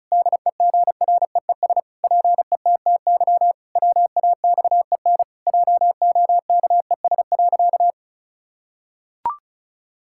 Morse code, heard as a keyed tone, about 36 wpm, 700 Hz. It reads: DEGREES PETTY WAXEN JOKES.
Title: Pangram Challenge v2 - 35wpm